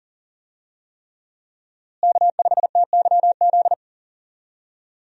Morse code, heard as a keyed tone, about 40 wpm, 700 Hz. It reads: K5TYZ
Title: USA Call Signs - 5 Characters - Repeated 3x - 40wpm